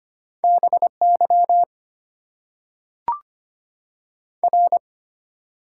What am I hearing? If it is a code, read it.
Morse code, heard as a keyed tone, about 25 wpm, 690 Hz. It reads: BY R
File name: Top 90 QSO Elements - 6x Character Spacing - 25wpm